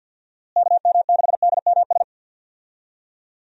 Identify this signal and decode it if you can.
Morse code, heard as a keyed tone, about 47 wpm, 700 Hz. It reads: KM6DMS
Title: USA Call Signs - 6 Characters - Repeated 3x - 50wpm